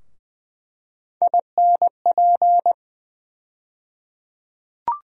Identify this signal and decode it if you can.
Morse code, heard as a keyed tone, about 20 wpm, 700 Hz. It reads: INP
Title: USA Call Sign Postfixes - 3 Letters 20wpm